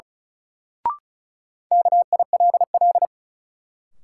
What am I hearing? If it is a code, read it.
Morse code, heard as a keyed tone, about 34 wpm, 700 Hz. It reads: KILL
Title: Sentences from Top 1400 Words - New Word Drill Alt - 35wpm